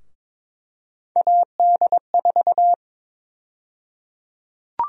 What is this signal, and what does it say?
Morse code, heard as a keyed tone, about 22 wpm, 700 Hz. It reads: AD4
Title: USA Call Sign Prefixes - 22wpm